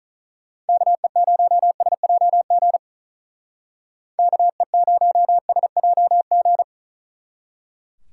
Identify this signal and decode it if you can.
Morse code, heard as a keyed tone, about 45 wpm, 700 Hz. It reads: KE0SJG KE0SJG